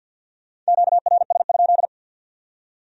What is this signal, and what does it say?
Morse code, heard as a keyed tone, about 65 wpm, spaced as for 56 wpm, 700 Hz. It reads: QRS?
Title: Speed-Racing Q-Codes - 50wpm